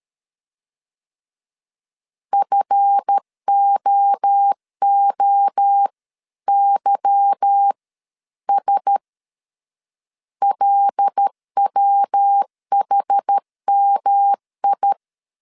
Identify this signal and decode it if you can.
Morse code, heard as a keyed tone, about 13 wpm, spaced as for 11 wpm, 800 Hz. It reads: FOOYS LWHMI
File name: CW / DET / 1 9 1500C S A 12wpm 3800